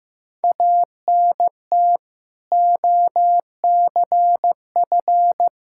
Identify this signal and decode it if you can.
Morse code, heard as a keyed tone, about 15 wpm, 700 Hz. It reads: ANT OCF
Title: Practice QSO - Exchange 2 - Info - 15wpm